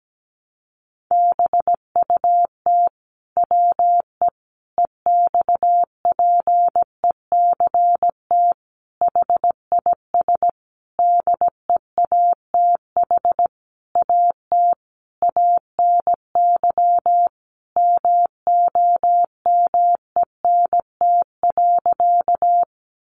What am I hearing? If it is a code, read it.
Morse code, heard as a keyed tone, about 17 wpm, 700 Hz. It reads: BUT WE EXPECT HIS DEATH AT ANY MOMENT.
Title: Sentences from Top 1100 Words 17wpm